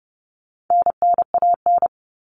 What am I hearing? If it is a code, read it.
Morse code, heard as a keyed tone, about 30 wpm, 700 Hz. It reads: NNAN
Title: All Cut Numbers - 4-Digit 30wpm